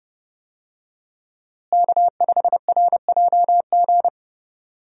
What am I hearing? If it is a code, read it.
Morse code, heard as a keyed tone, about 28 wpm, 700 Hz. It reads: K5RJG
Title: USA Call Signs - 5 Characters - Repeated 3x - 30wpm